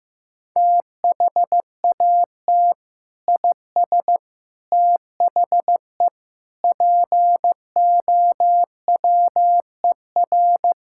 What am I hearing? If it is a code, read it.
Morse code, heard as a keyed tone, about 15 wpm, 700 Hz. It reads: THAT IS THE POWER
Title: Sentences from Top 1000 Words - Review Part 5 - 15wpm